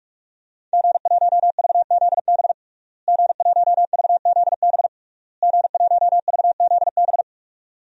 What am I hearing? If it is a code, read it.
Morse code, heard as a keyed tone, about 46 wpm, 700 Hz. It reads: G1VZB G1VZB G1VZB
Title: DXCC-100 Call Signs - 5 Characters - Repeated 3x - 45wpm